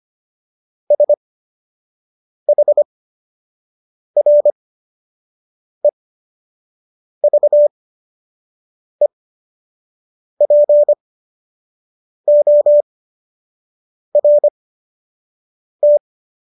Morse code, heard as a keyed tone, about 25 wpm, spaced as for 6 wpm, 590 Hz. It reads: SHREVEPORT